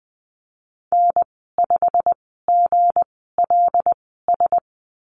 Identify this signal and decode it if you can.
Morse code, heard as a keyed tone, about 20 wpm, spaced as for 14 wpm, 700 Hz. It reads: N5GLS
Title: 167 Lesson 27g - Number 7 - Call Signs 20wpm